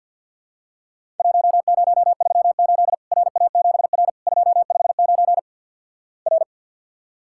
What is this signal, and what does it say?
Morse code, heard as a keyed tone, about 47 wpm, 700 Hz. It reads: 1038 RA7R 259 R
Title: Worked All Europe DX Contest - QTC Exchange with Repeats - 50wpm